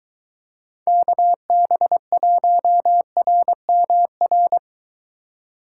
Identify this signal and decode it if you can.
Morse code, heard as a keyed tone, about 23 wpm, 700 Hz. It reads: KB1RMR